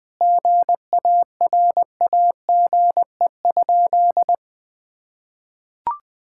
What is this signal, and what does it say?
Morse code, heard as a keyed tone, about 20 wpm, 700 Hz. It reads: GARAGE?
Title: Sentences from Top 1600 Words 20wpm